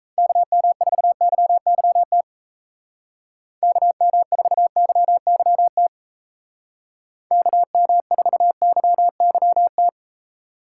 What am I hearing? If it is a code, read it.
Morse code, heard as a keyed tone, about 34 wpm, 700 Hz. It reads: KM4YYT KM4YYT KM4YYT
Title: Speed Racing - USA 2x3 Call Signs 28wpm